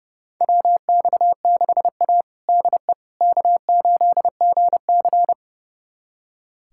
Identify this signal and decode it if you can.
Morse code, heard as a keyed tone, about 32 wpm, 700 Hz. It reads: WX6A DE K8GC
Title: USA Callsign DE Callsign - 4 Characters - 30wpm